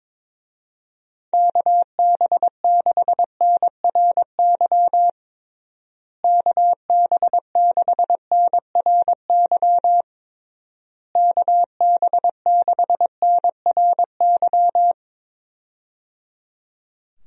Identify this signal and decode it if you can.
Morse code, heard as a keyed tone, about 22 wpm, 700 Hz. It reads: KB6NRY KB6NRY KB6NRY
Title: USA Call Signs - 6 Characters - Repeated 3x - 22wpm